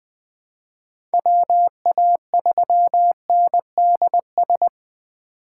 Morse code, heard as a keyed tone, about 20 wpm, 700 Hz. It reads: WA3NDS
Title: USA Call Signs - 6 Characters - 3x Character Spacing - 20wpm